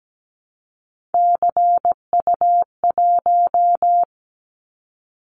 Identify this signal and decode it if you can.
Morse code, heard as a keyed tone, about 17 wpm, 700 Hz. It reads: CU1
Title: Callsign Prefixes for Top 100 DXCC - 17wpm